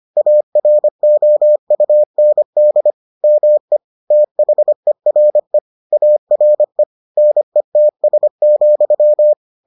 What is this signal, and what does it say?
Morse code, heard as a keyed tone, about 25 wpm, 600 Hz. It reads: AROUND ME THERE ARE NETS,